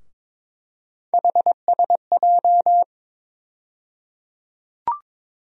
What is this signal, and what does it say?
Morse code, heard as a keyed tone, about 22 wpm, 700 Hz. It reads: HSJ
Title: USA Call Sign Postfixes - 3 Letters 22wpm